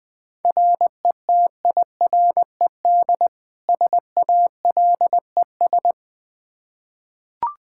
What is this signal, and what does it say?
Morse code, heard as a keyed tone, about 20 wpm, 700 Hz. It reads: RETIRED SALES
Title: Practice QSO - Exchange 3 - More Info - 20wpm